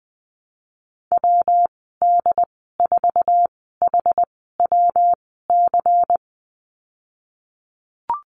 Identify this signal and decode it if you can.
Morse code, heard as a keyed tone, about 20 wpm, spaced as for 14 wpm, 700 Hz. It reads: WD4HWC
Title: 122 Lesson 20g - Number 3 - Call Signs 20wpm